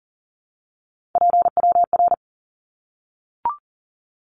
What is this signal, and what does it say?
Morse code, heard as a keyed tone, about 39 wpm, 700 Hz. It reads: PWR
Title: Abbreviations Repeated 3x - 40wpm